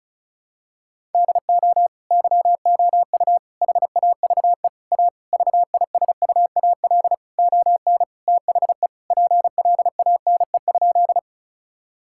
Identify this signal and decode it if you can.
Morse code, heard as a keyed tone, about 34 wpm, 700 Hz. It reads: DO YOU HAVE A VISUAL ON THE PLANE?
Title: Sentences from Top 1000 Words - Review Part 6 - 35wpm